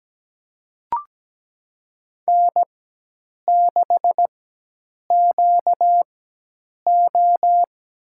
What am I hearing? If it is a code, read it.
Morse code, heard as a keyed tone, about 17 wpm, spaced as for 8 wpm, 700 Hz. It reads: N6QO
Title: USA Call Signs - 4x Character Spacing - 17wpm